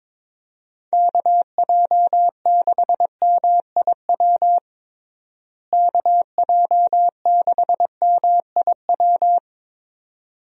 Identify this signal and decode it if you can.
Morse code, heard as a keyed tone, about 22 wpm, 700 Hz. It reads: KJ6MIW KJ6MIW
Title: USA Call Signs - 6 Characters - Repeated 2x - 22wpm